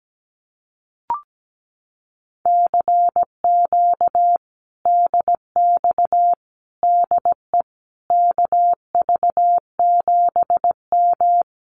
Morse code, heard as a keyed tone, about 17 wpm, 700 Hz. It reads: CQ DX DE KV7M
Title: Common QSO Phrases Encore 17wpm